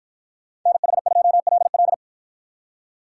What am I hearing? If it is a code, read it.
Morse code, heard as a keyed tone, about 53 wpm, 700 Hz. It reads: NH2LL